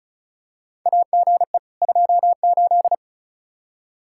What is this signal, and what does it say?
Morse code, heard as a keyed tone, about 36 wpm, 700 Hz. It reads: AGE 28
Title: K1USN SST QSO Practice 35wpm